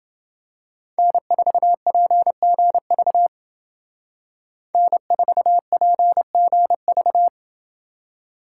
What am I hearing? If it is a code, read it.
Morse code, heard as a keyed tone, about 28 wpm, 700 Hz. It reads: N4PGV N4PGV